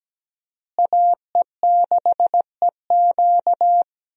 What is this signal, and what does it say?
Morse code, heard as a keyed tone, about 17 wpm, 700 Hz. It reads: AE6EQ
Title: USA Call Signs - 5 Characters - Repeated 3x - 17wpm